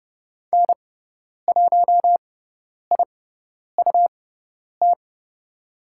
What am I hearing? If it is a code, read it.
Morse code, heard as a keyed tone, about 30 wpm, spaced as for 10 wpm, 700 Hz. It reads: N1IUT
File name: Lesson 18g - Letter m - Call Signs 30wpm